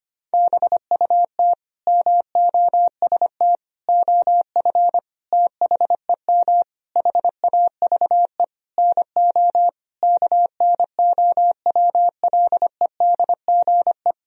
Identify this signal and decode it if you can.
Morse code, heard as a keyed tone, about 25 wpm, 700 Hz. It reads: BUT MOST OF THEM HAVE NO KNOWLEDGE
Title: Sentences from Top 1000 Words - Review Part 3 - 25wpm